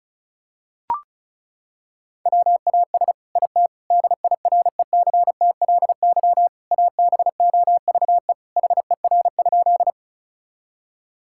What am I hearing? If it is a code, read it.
Morse code, heard as a keyed tone, about 36 wpm, 700 Hz. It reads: WAS IT DIRECTLY ABOVE HER?
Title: Sentences from Top 1000 Words - Review Part 5 - 35wpm